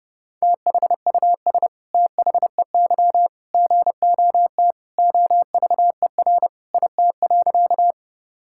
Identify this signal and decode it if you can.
Morse code, heard as a keyed tone, about 30 wpm, 700 Hz. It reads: THUS THEY GOT OVER IT.
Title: Sentences from Top 1000 Words - Review Part 5 - 30wpm